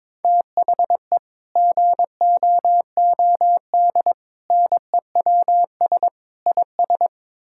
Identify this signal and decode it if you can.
Morse code, heard as a keyed tone, about 22 wpm, 700 Hz. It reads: THE GOOD NEWS IS
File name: practice / 22wpm / Build A Phrase - 5-Word Phrases - 22wpm